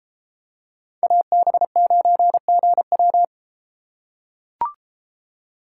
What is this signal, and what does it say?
Morse code, heard as a keyed tone, about 32 wpm, 700 Hz. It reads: AB9GW